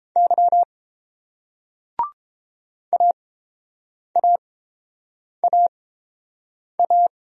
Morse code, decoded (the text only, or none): Y AAAA